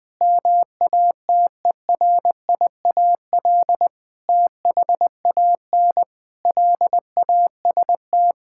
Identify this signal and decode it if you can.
Morse code, heard as a keyed tone, about 20 wpm, 700 Hz. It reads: MATERIAL THAN LAST
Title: Sentences from Top 1000 Words - Review Part 5 - 20wpm